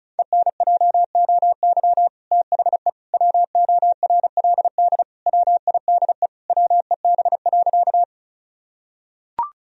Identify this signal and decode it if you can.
Morse code, heard as a keyed tone, about 36 wpm, 700 Hz. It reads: ENJOY THE WORLD WIDE WEB.
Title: Sentences from Top 1000 Words - Review Part 4 - 35wpm